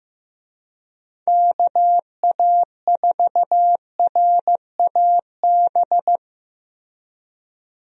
Morse code, heard as a keyed tone, about 15 wpm, 700 Hz. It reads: KA4RAB
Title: USA Call Signs - 6 Characters - Repeated 2x - 15wpm